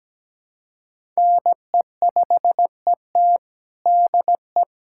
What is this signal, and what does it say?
Morse code, heard as a keyed tone, about 17 wpm, 700 Hz. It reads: NE5ET DE
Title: USA Callsign DE Callsign - 5 Characters - 17wpm